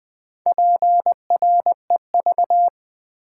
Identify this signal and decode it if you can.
Morse code, heard as a keyed tone, about 20 wpm, 700 Hz. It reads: PREV
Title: Build A Word - 4 Letters At A Time - 10-Letter Words - 20wpm